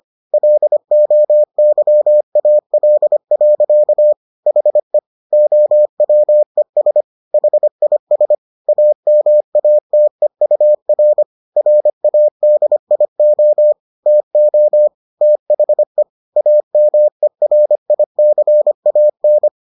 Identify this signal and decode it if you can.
Morse code, heard as a keyed tone, about 25 wpm, 600 Hz. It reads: LOYAL. HE OWES HIS AMATEUR RADIO TO THE AMERICAN